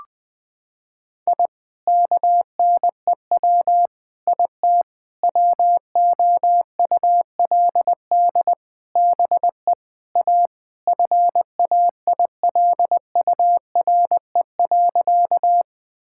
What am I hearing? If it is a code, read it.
Morse code, heard as a keyed tone, about 20 wpm, 700 Hz. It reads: I KNEW IT WOULD BE A FAILURE.
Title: Sentences from Top 1000 Words - Review Part 6 - 20wpm